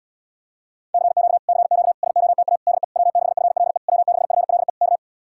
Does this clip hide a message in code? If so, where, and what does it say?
Yes: Morse code from 0.9 to 5.0 s, heard as a keyed tone, about 79 wpm, 700 Hz. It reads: CQ CQ SOTA DE W6FLE W6FLE K